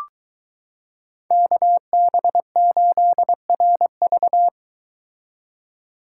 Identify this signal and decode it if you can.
Morse code, heard as a keyed tone, about 23 wpm, 700 Hz. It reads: KB8RV